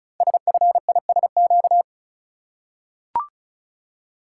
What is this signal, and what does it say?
Morse code, heard as a keyed tone, about 36 wpm, 700 Hz. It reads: SFISQ
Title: Code Groups - 5 Letter:Numbers - 35wpm